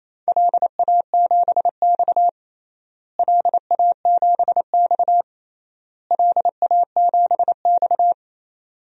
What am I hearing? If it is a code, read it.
Morse code, heard as a keyed tone, about 28 wpm, 700 Hz. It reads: LA7X LA7X LA7X
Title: DXCC-100 Call Signs - 4 Characters - Repeated 3x - 28wpm